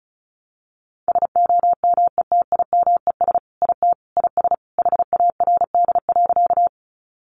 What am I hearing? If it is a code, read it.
Morse code, heard as a keyed tone, about 34 wpm, 700 Hz. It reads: SOMETIMES IT IS HARD.
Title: Sentences from Top 600 Words 35wpm